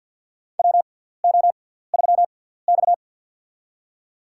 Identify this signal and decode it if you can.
Morse code, heard as a keyed tone, about 46 wpm, spaced as for 17 wpm, 700 Hz. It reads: WO3X
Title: USA Call Signs - 4 Characters - 6x Character Spacing - 50wpm